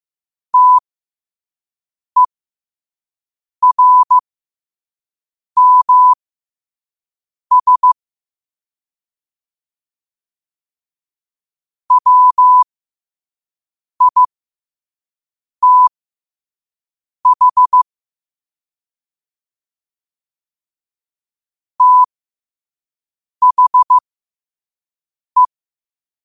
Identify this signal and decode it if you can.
Morse code, heard as a keyed tone, about 15 wpm, spaced as for 5 wpm, 1000 Hz. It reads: TERMS WITH THE
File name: qotdfiles / 5wpm